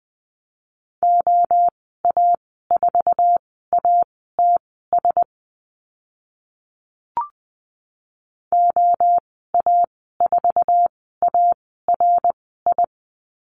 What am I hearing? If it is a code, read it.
Morse code, heard as a keyed tone, about 20 wpm, spaced as for 14 wpm, 700 Hz. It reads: OA4ATS OA4ARI
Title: 069 Lesson 12g - Letter d - Call Signs 20wpm